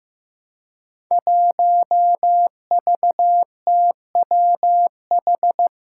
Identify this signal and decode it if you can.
Morse code, heard as a keyed tone, about 15 wpm, 700 Hz. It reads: 1VTWH
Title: Five Challenging Letter-Number Combinations 15wpm